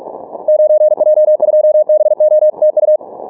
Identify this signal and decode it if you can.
Morse code, heard as a keyed tone, about 43 wpm, 610 Hz. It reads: 9J2BO TU
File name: training / morserunner / Example